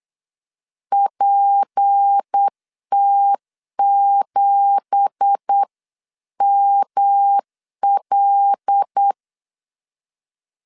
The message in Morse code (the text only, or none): PT7ML